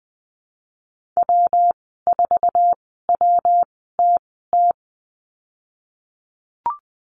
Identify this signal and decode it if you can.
Morse code, heard as a keyed tone, about 20 wpm, spaced as for 15 wpm, 700 Hz. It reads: W4WTT
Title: 122 Lesson 20g - Number 3 - Call Signs 20wpm